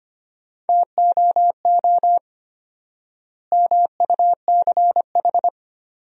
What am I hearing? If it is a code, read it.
Morse code, heard as a keyed tone, about 25 wpm, 700 Hz. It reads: TOO MUCH
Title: 4x Word-Spacing - Sentences from Top 200 Words - 25wpm